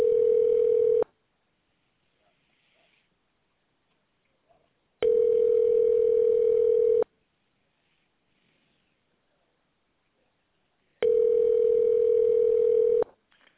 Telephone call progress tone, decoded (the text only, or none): ringback tone